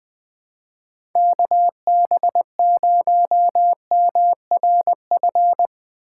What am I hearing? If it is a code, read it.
Morse code, heard as a keyed tone, about 21 wpm, 700 Hz. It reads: KB0MRF
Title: USA Call Signs - 6 Characters - 3x Character Spacing - 20wpm